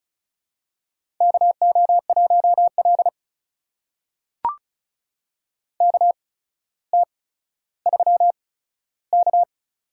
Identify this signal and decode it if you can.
Morse code, heard as a keyed tone, about 36 wpm, spaced as for 15 wpm, 700 Hz. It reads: KO1L KT3K
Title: USA Call Signs - 4 Characters - 8x Character Spacing - 35wpm